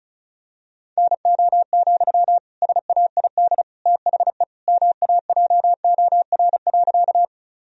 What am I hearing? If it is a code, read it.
Morse code, heard as a keyed tone, about 34 wpm, 700 Hz. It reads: NO, SAID THE MAJOR.
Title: Sentences from Top 1000 Words - Review Part 4 - 35wpm